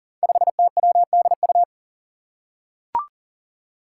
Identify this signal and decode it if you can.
Morse code, heard as a keyed tone, about 40 wpm, 700 Hz. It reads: 5TWDU
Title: Five Challenging Letter-Number Combinations 40wpm